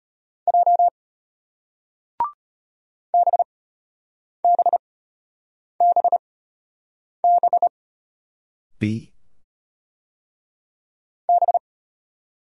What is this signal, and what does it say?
Morse code, heard as a keyed tone, about 34 wpm, spaced as for 8 wpm, 700 Hz. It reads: J BBBB B